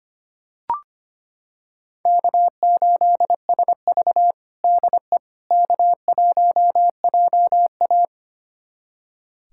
Morse code, heard as a keyed tone, about 25 wpm, 700 Hz. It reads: K8SV DE K1JA
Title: USA Callsign DE Callsign - 4 Characters - 25wpm